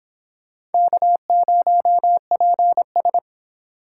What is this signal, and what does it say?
Morse code, heard as a keyed tone, about 32 wpm, 700 Hz. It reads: K0PS